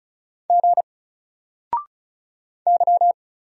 Morse code, heard as a keyed tone, about 35 wpm, 700 Hz. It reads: G Y
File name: Single-Letter-Number-35wpm